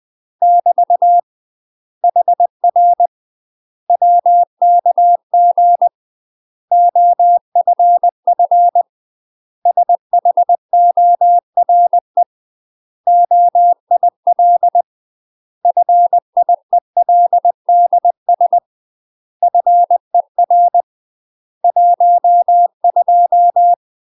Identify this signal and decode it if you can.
Morse code, heard as a keyed tone, about 20 wpm, 700 Hz. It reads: <BT> HR WKG OFF SHORE OIL FIELDS FER 12